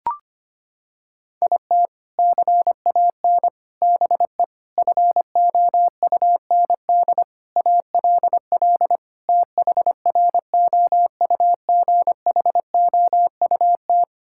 Morse code, heard as a keyed tone, about 25 wpm, 700 Hz. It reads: IT CAN BE FOUND ALL THROUGHOUT
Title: Sentences from Top 1000 Words - Review Part 5 - 25wpm